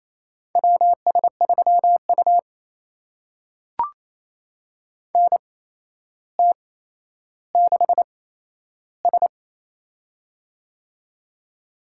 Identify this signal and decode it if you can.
Morse code, heard as a keyed tone, about 27 wpm, spaced as for 12 wpm, 700 Hz. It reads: WS3U NT6S
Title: USA Call Signs - 4 Characters - 8x Character Spacing - 28wpm